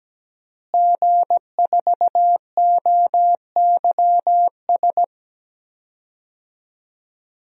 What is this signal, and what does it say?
Morse code, heard as a keyed tone, about 17 wpm, 700 Hz. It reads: G4OYS
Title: DXCC-100 Call Signs - 5 Characters - Repeated 2x - 17wpm